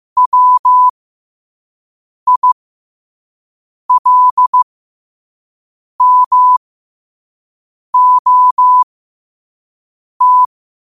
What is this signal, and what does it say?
Morse code, heard as a keyed tone, about 15 wpm, spaced as for 5 wpm, 1000 Hz. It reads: WILMOT
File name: 5wpm